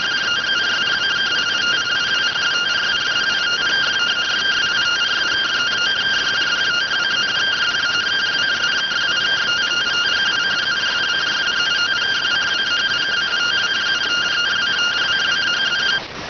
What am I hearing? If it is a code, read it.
Two-tone teleprinter signal, about 170 Hz shift.